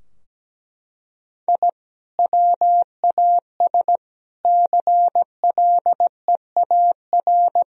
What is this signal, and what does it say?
Morse code, heard as a keyed tone, about 17 wpm, 700 Hz. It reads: I WAS CLEAR
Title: Sentences from Top 1000 Words - Review Part 2 - 17wpm